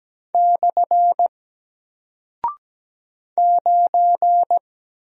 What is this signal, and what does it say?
Morse code, heard as a keyed tone, about 17 wpm, 700 Hz. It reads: / 9
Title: Single Character (Letter:Number:Punctuation:Prosign) Alt 17wpm